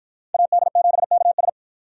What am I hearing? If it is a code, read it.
Morse code, heard as a keyed tone, about 53 wpm, 700 Hz. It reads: AD7KS